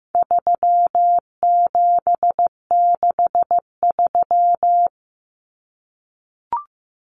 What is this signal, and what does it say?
Morse code, heard as a keyed tone, about 15 wpm, 700 Hz. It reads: 3763
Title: Common Cut Numbers - 4-Digit 15wpm